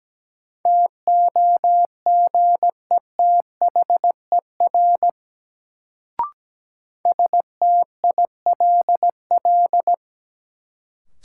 Morse code, heard as a keyed tone, about 17 wpm, 700 Hz. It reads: TOGETHER STILL
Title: Sentences from Top 300 Words - New Word Drill - 17wpm